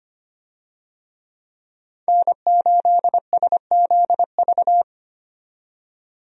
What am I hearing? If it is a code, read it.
Morse code, heard as a keyed tone, about 25 wpm, 700 Hz. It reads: N8SZV
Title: USA Call Signs - 5 Characters - 25wpm